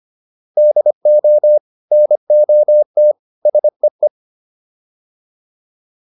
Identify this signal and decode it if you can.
Morse code, heard as a keyed tone, about 25 wpm, 600 Hz. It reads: DO NOT SEE